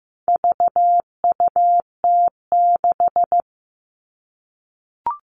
Morse code, heard as a keyed tone, about 15 wpm, 700 Hz. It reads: VUT6
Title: All Cut Numbers - 4-Digit 15wpm